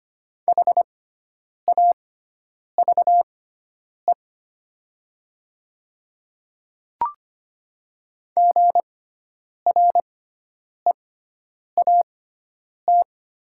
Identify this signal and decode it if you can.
Morse code, heard as a keyed tone, about 25 wpm, spaced as for 9 wpm, 700 Hz. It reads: HAVE GREAT
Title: 6x Character Spacing - Top 200 Words - 25wpm